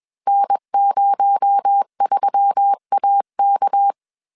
Morse code, heard as a keyed tone, about 21 wpm, 800 Hz. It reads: N03AK
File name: CW / DET / 1 6 1500C S B 20wpm 2048